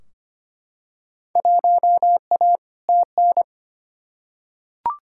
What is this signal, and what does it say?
Morse code, heard as a keyed tone, about 25 wpm, 700 Hz. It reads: 1A TN
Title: Field Day Exchanges - 25wpm